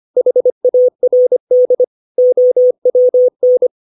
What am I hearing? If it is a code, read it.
Morse code, heard as a keyed tone, about 25 wpm, 500 Hz. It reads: HARD OWN
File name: 632 words / 632 Words 25WPM